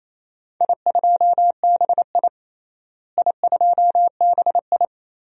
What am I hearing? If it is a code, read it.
Morse code, heard as a keyed tone, about 28 wpm, 700 Hz. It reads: I2BI I2BI
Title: DXCC-100 Call Signs - 4 Characters - Repeated 2x - 28wpm